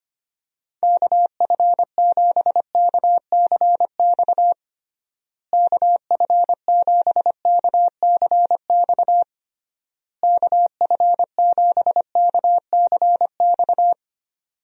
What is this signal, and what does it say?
Morse code, heard as a keyed tone, about 25 wpm, 700 Hz. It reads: KF7KCX KF7KCX KF7KCX
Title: USA Call Signs - 6 Characters - Repeated 3x - 25wpm